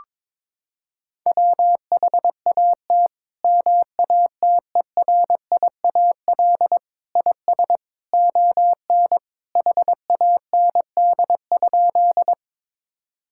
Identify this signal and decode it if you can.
Morse code, heard as a keyed tone, about 22 wpm, 700 Hz. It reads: WHAT MATERIAL IS ON HAND?
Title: Sentences from Top 1000 Words - Review Part 5 - 22wpm